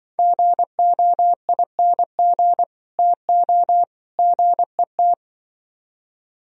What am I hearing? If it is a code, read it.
Morse code, heard as a keyed tone, about 25 wpm, 700 Hz. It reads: GOING TO GET
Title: Speed-Racing - Sets of 3 Words 20wpm